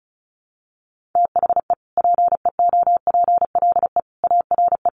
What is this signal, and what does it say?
Morse code, heard as a keyed tone, about 34 wpm, 700 Hz. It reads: THE PEOPLE ARE
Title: Sentences from Top 1300 Words 35wpm